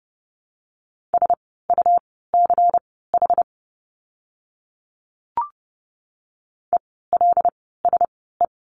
Morse code, heard as a keyed tone, about 30 wpm, spaced as for 17 wpm, 700 Hz. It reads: SUCH ELSE
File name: Lesson 17f - Letter u - Words 30wpm